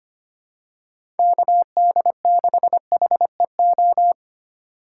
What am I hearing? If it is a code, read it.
Morse code, heard as a keyed tone, about 25 wpm, 700 Hz. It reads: KD6HEO
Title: USA Call Signs - 6 Characters - 4x Character Spacing - 25wpm